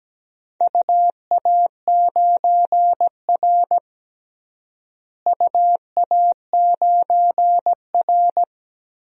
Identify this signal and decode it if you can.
Morse code, heard as a keyed tone, about 17 wpm, 700 Hz. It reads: UA9R UA9R
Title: DXCC-100 Call Signs - 4 Characters - Repeated 3x - 17wpm